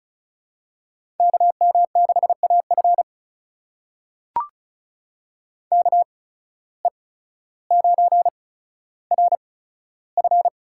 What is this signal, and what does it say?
Morse code, heard as a keyed tone, about 34 wpm, spaced as for 15 wpm, 700 Hz. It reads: KM6AF KE9RF
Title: USA Call Signs - 5 Characters - 8x Character Spacing - 35wpm